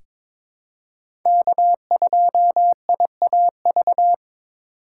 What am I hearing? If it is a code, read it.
Morse code, heard as a keyed tone, about 22 wpm, 700 Hz. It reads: K2IAV
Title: USA Call Signs - 5 Characters - 3x Character Spacing - 22wpm